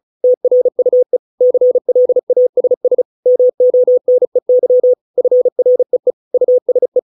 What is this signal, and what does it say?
Morse code, heard as a keyed tone, about 34 wpm, 500 Hz. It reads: TRUE CLASS MONEY FREE USE